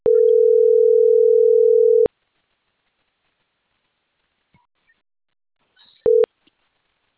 Ringback tone (North America), 440 + 480 Hz.